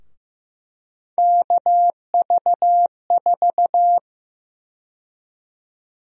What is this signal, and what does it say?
Morse code, heard as a keyed tone, about 15 wpm, 700 Hz. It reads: KV4
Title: USA Call Sign Prefixes - Repeated 2x - 15wpm